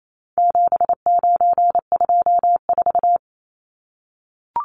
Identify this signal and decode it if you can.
Morse code, heard as a keyed tone, about 28 wpm, 700 Hz. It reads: 7924
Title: Numbers - 4-Digit 28wpm